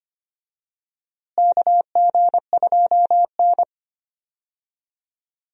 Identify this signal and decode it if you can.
Morse code, heard as a keyed tone, about 25 wpm, 700 Hz. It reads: KG2N
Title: USA Call Signs - 4 Characters - Repeated 3x - 25wpm